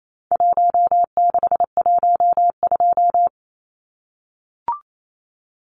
Morse code, heard as a keyed tone, about 28 wpm, 700 Hz. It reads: 1612
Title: Numbers - Contest Style - 4-Digit 28wpm